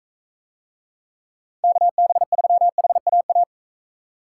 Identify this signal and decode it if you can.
Morse code, heard as a keyed tone, about 39 wpm, 700 Hz. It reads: KB3HAA